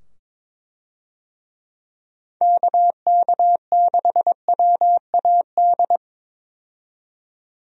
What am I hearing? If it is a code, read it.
Morse code, heard as a keyed tone, about 22 wpm, 700 Hz. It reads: KK6WAD
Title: USA Call Signs - 6 Characters - 22wpm